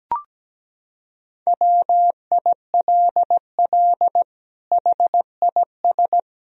Morse code, heard as a keyed tone, about 17 wpm, 700 Hz. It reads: WILL HIS
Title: Sentences from Top 1000 Words - Review Part 4 - 17wpm